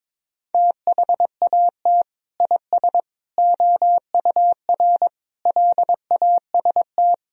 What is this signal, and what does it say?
Morse code, heard as a keyed tone, about 22 wpm, 700 Hz. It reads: THAT IS OUR LAST
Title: Sentences from Top 1000 Words - Review Part 4 - 22wpm